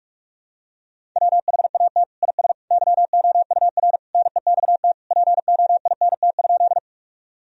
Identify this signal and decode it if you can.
Morse code, heard as a keyed tone, about 46 wpm, 700 Hz. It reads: WHAT IS YOUR NEXT POINT?
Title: Sentences from Top 1000 Words - Review Part 1 - 45wpm